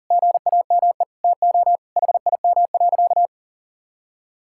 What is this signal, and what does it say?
Morse code, heard as a keyed tone, about 37 wpm, 700 Hz. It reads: GAME TO HIM.
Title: Sentences from Top 1000 Words - Review Part 5 - 40wpm